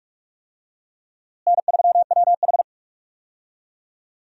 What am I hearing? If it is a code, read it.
Morse code, heard as a keyed tone, about 46 wpm, 700 Hz. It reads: N3WH
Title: USA Call Signs - 4 Characters - Repeated 3x - 45wpm